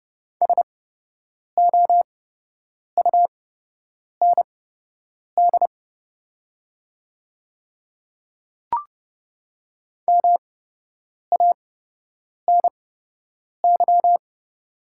Morse code, heard as a keyed tone, about 30 wpm, spaced as for 8 wpm, 700 Hz. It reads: SOUND MANY
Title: 8x Character Spacing - Top 100 Words - 30wpm